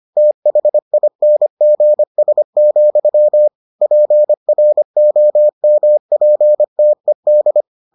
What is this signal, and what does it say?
Morse code, heard as a keyed tone, about 25 wpm, 600 Hz. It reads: THINGS, PROMPTED